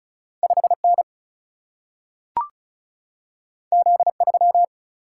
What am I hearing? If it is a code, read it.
Morse code, heard as a keyed tone, about 36 wpm, 700 Hz. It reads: 5N Z3
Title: Single-Letter Single-Number 35wpm